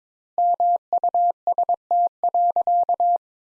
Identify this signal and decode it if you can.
Morse code, heard as a keyed tone, about 22 wpm, 700 Hz. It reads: MUST.
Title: Sentences Top 200 Words - 22wpm